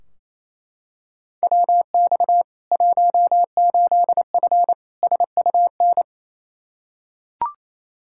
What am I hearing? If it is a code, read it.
Morse code, heard as a keyed tone, about 28 wpm, 700 Hz. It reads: WX 18F SUN
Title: Practice QSO - Exchange 2 - Info - 28wpm